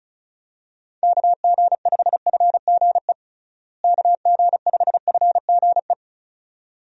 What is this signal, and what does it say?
Morse code, heard as a keyed tone, about 34 wpm, 700 Hz. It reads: KG5FGE KG5FGE
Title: USA Call Signs - 6 Characters - Repeated 2x - 35wpm